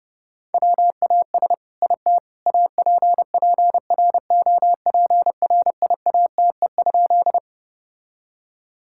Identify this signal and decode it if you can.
Morse code, heard as a keyed tone, about 32 wpm, 700 Hz. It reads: WAS IT APPROPRIATE?
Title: Sentences from Top 1000 Words - Review Part 5 - 30wpm